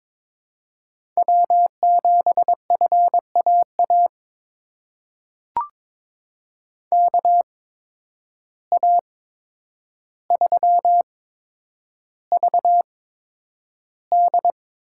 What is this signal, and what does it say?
Morse code, heard as a keyed tone, about 22 wpm, spaced as for 9 wpm, 700 Hz. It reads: W7FAA KA3VD
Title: USA Call Signs - 8x Character Spacing - 22wpm